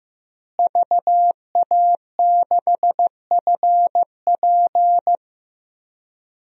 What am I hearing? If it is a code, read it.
Morse code, heard as a keyed tone, about 15 wpm, 700 Hz. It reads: VA6FP
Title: DXCC-100 Call Signs - 5 Characters - Repeated 2x - 15wpm